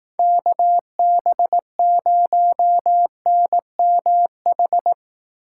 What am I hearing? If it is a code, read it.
Morse code, heard as a keyed tone, about 18 wpm, 700 Hz. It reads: KB0NMH
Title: Speed Racing - USA 2x3 Call Signs 15wpm